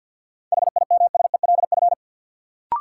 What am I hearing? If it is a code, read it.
Morse code, heard as a keyed tone, about 52 wpm, 700 Hz. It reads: HIMSELF
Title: Sentences from Top 1500 Words - New Word Drill - 50wpm